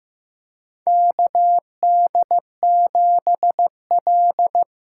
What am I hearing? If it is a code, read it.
Morse code, heard as a keyed tone, about 15 wpm, 700 Hz. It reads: KD7L
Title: USA Call Signs - 6 Characters - 6x Character Spacing - 15wpm